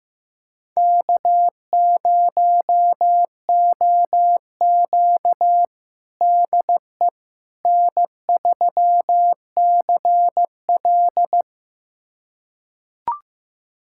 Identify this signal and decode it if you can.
Morse code, heard as a keyed tone, about 15 wpm, 700 Hz. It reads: K0OQ DE N3CL
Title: USA Callsign DE Callsign - 4 Characters - 15wpm